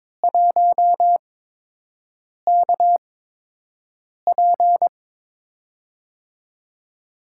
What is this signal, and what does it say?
Morse code, heard as a keyed tone, about 22 wpm, spaced as for 6 wpm, 700 Hz. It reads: 1KP
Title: USA Call Signs - 5 Characters - 8x Character Spacing - 22wpm